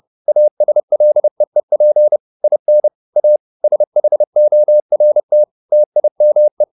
Morse code, heard as a keyed tone, about 30 wpm, 600 Hz. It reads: ASLEEP IN A SHORT TIME
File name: the old man and the sea / the old man and the sea 1 30wpm